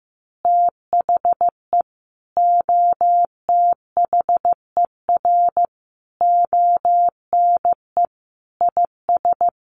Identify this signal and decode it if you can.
Morse code, heard as a keyed tone, about 15 wpm, 700 Hz. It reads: THE OTHER ONE IS
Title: Additional Easy Phrases 15wpm